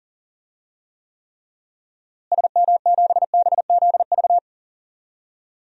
Morse code, heard as a keyed tone, about 40 wpm, 700 Hz. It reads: SM7BZV
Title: DXCC-100 Call Signs - 6 Characters - Repeated 3x - 40wpm